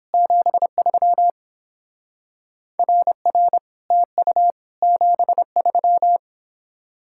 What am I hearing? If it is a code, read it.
Morse code, heard as a keyed tone, about 29 wpm, 700 Hz. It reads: 73 RR TU 73